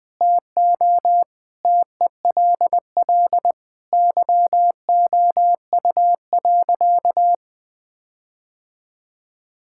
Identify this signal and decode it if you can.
Morse code, heard as a keyed tone, about 20 wpm, 700 Hz. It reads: TO TELL YOU.
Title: Sentences from Top 1600 Words 20wpm